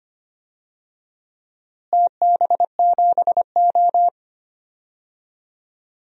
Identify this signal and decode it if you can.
Morse code, heard as a keyed tone, about 25 wpm, 700 Hz. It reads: TB7O
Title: DXCC-100 Call Signs - 4 Characters - Repeated 3x - 25wpm